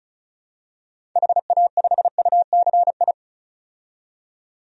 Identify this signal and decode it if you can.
Morse code, heard as a keyed tone, about 36 wpm, 700 Hz. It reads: HA5UCI
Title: DXCC-100 Call Signs - 6 Characters - 35wpm